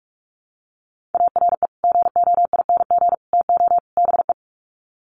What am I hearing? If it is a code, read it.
Morse code, heard as a keyed tone, about 44 wpm, 700 Hz. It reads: ARE GOING TO BE
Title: Sets of 4 Words 45wpm